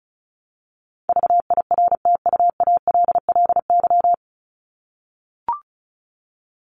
Morse code, heard as a keyed tone, about 34 wpm, 700 Hz. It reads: VIRTUALLY
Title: Top 1501-1750 Words 35wpm